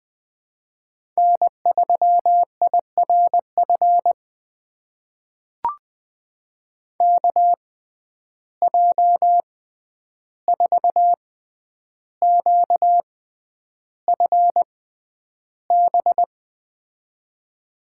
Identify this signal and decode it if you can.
Morse code, heard as a keyed tone, about 20 wpm, spaced as for 10 wpm, 700 Hz. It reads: N3IRF KJ4QFB
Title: USA Call Signs - 6x Character Spacing - 20wpm